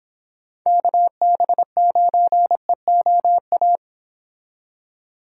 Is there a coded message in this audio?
Yes: Morse code from 0.7 to 3.8 s, heard as a keyed tone, about 26 wpm, 700 Hz. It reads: KB9EOA